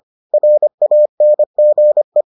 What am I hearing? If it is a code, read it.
Morse code, heard as a keyed tone, about 25 wpm, 600 Hz. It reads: RANGE